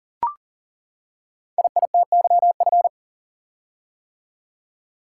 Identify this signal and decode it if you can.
Morse code, heard as a keyed tone, about 38 wpm, 690 Hz. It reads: IITYF
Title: Code Groups - 5 Letters - 40wpm